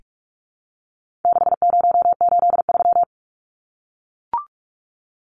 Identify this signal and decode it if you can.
Morse code, heard as a keyed tone, about 44 wpm, 700 Hz. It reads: 6083
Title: Numbers - 4-Digit 45wpm